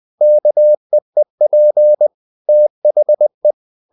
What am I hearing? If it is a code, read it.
Morse code, heard as a keyed tone, about 20 wpm, 600 Hz. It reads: KEEP THE
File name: the old man and the sea part 2 20wpm